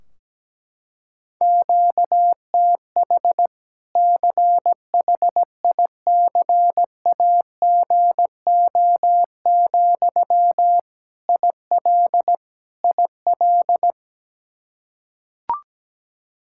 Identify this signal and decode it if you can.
Morse code, heard as a keyed tone, about 17 wpm, 700 Hz. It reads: QTH CHICAGO, IL IL
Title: Practice QSO - Exchange 1 - Beginning - 17wpm